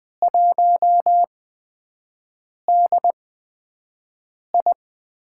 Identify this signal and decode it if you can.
Morse code, heard as a keyed tone, about 20 wpm, spaced as for 5 wpm, 700 Hz. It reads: 1DI
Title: USA Call Signs - 4 Characters - 8x Character Spacing - 20wpm